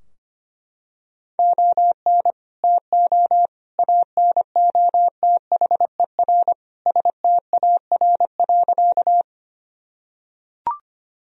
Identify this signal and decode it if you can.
Morse code, heard as a keyed tone, about 25 wpm, 700 Hz. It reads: ON TO ANOTHER STAR.
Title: Sentences from Top 1000 Words - Review Part 2 - 25wpm